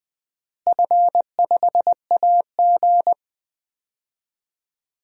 Morse code, heard as a keyed tone, about 20 wpm, 700 Hz. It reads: F5AG
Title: DXCC-100 Call Signs - 4 Characters - Repeated 3x - 20wpm